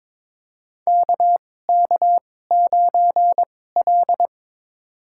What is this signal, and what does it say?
Morse code, heard as a keyed tone, about 22 wpm, spaced as for 16 wpm, 700 Hz. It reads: KK9L
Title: USA Call Signs - 4 Characters - 2x Character Spacing - 22wpm